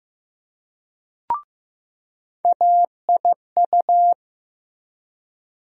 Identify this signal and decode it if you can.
Morse code, heard as a keyed tone, about 15 wpm, 700 Hz. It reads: AIU
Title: USA Call Sign Postfixes - 3 Letters 15wpm